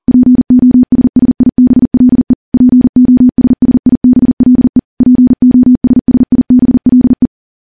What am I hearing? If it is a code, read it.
Morse code, heard as a keyed tone, about 40 wpm, 250 Hz. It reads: POSSIBLE POSSIBLE POSSIBLE
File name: morse / morse-training-40WPM